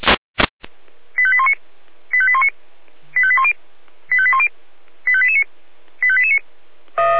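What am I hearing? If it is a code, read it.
Five-tone selective call (ZVEI), heard as quick runs of five tones, tones of about 70 ms. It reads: 85819 85819 85810 85810 85889 85889
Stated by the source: ZVEI five-tone call received by SDR.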